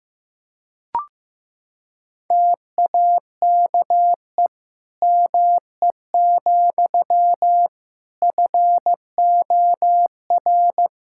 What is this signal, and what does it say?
Morse code, heard as a keyed tone, about 15 wpm, 700 Hz. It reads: TAKE ME, FOR
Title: Sentences from Top 1000 Words - Review Part 5 - 15wpm